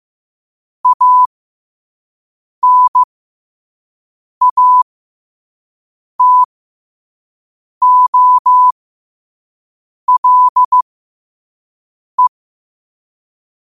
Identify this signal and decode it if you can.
Morse code, heard as a keyed tone, about 15 wpm, spaced as for 5 wpm, 1000 Hz. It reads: ANATOLE
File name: qotdfiles / 5wpm